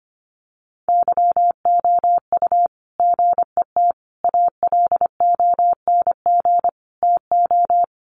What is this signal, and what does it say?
Morse code, heard as a keyed tone, about 25 wpm, 700 Hz. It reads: YOU GET ALONG TO
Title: Sentences from Top 700 Words 25wpm